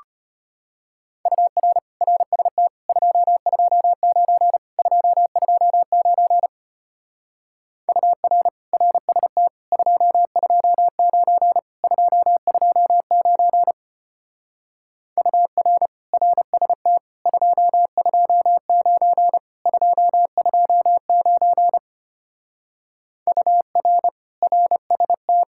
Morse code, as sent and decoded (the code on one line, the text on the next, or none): ..- .-. / .-. ... - / ..--- ..--- ----. / ..--- ..--- ----. / ..- .-. / .-. ... - / ..--- ..--- ----. / ..--- ..--- ----. / ..- .-. / .-. ... - / ..--- ..--- ----. / ..--- ..--- ----. / ..- .-. / .-. ... -
UR RST 229 229 UR RST 229 229 UR RST 229 229 UR RST